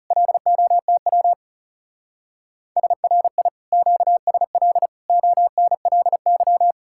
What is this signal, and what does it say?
Morse code, heard as a keyed tone, about 37 wpm, 700 Hz. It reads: LOTW SRI QSL ONLY
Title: Speed-Racing Common QSO Phrases - 30wpm